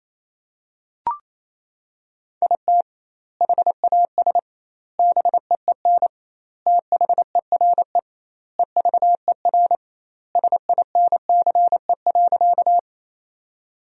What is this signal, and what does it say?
Morse code, heard as a keyed tone, about 28 wpm, 700 Hz. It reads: IT HAS BEEN THERE EVER SINCE.
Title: 2x Word-Spacing - Sentences from Top 400 Words - 28wpm